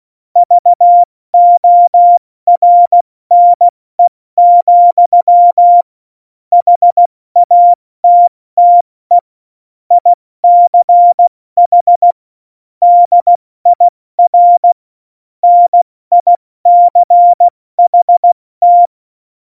Morse code, heard as a keyed tone, about 16 wpm, 700 Hz. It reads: VORNE, HATTE ICH DIR NICHT
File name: Morsekurs